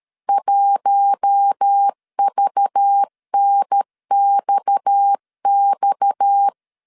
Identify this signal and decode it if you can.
Morse code, heard as a keyed tone, about 13 wpm, 800 Hz. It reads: 1VNXX
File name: LEZIONI / 6 PKX0 / 6 PKX 12wpm 2024